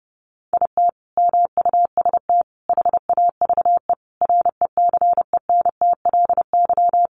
Morse code, heard as a keyed tone, about 30 wpm, 700 Hz. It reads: IT MUST HAVE RECENTLY
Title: Sentences from Top 1200 Words 30wpm